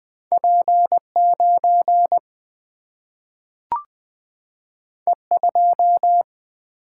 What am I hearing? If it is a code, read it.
Morse code, heard as a keyed tone, about 20 wpm, 700 Hz. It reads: P9 E2
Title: Single-Letter Single-Number 20wpm